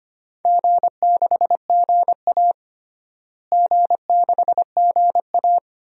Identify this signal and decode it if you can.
Morse code, heard as a keyed tone, about 25 wpm, 700 Hz. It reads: G6GA G6GA
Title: DXCC-100 Call Signs - 4 Characters - Repeated 2x - 25wpm